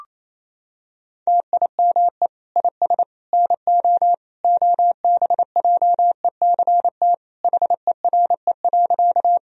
Morse code, heard as a keyed tone, about 29 wpm, 700 Hz. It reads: TIME IS NO OBJECT HERE.
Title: Sentences from Top 1000 Words - Review Part 3 - 28wpm